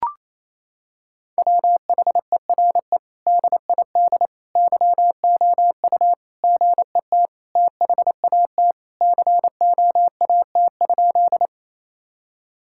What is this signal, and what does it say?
Morse code, heard as a keyed tone, about 28 wpm, 700 Hz. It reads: WHERE DID YOU GET THAT COAT?
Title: Sentences from Top 1600 Words 28wpm